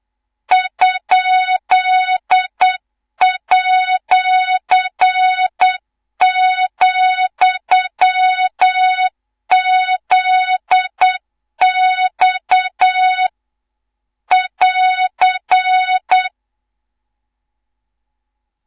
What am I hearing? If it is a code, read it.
Morse code, heard as a keyed tone, about 8 wpm, 3700 Hz, with noise. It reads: ?@,ZX <AR>
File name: Morze-HA2MN / hang / 18 IRASJELEK2